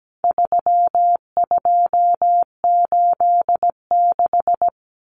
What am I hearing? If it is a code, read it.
Morse code, heard as a keyed tone, about 17 wpm, 700 Hz. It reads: 3286
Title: Common Cut Numbers - 4-Digit 17wpm